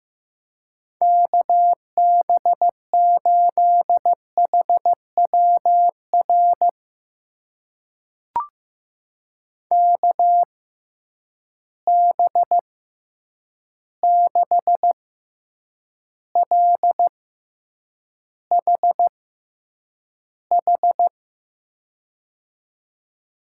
Morse code, heard as a keyed tone, about 15 wpm, spaced as for 8 wpm, 700 Hz. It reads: KB8HWR KB6LHH
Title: USA Call Signs - 6 Characters - 6x Character Spacing - 15wpm